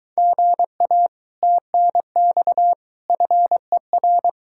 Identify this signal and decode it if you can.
Morse code, heard as a keyed tone, about 22 wpm, 700 Hz. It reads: GA TNX FER